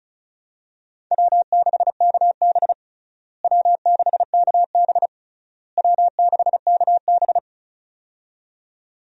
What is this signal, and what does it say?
Morse code, heard as a keyed tone, about 36 wpm, 700 Hz. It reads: W6KB W6KB W6KB
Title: USA Call Signs - 4 Characters - Repeated 3x - 35wpm